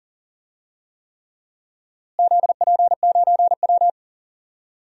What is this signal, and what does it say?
Morse code, heard as a keyed tone, about 38 wpm, 700 Hz. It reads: ZP9W
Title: DXCC-100 Call Signs - 4 Characters - Repeated 3x - 40wpm